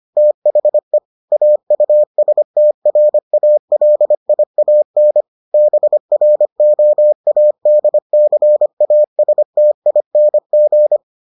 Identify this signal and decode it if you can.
Morse code, heard as a keyed tone, about 25 wpm, 600 Hz. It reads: THE AUSTRALIAN BROADCASTING